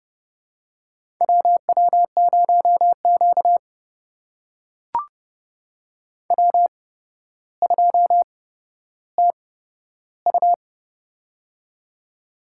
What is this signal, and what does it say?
Morse code, heard as a keyed tone, about 30 wpm, spaced as for 13 wpm, 700 Hz. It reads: WW0Q W2TU
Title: USA Call Signs - 4 Characters - 8x Character Spacing - 30wpm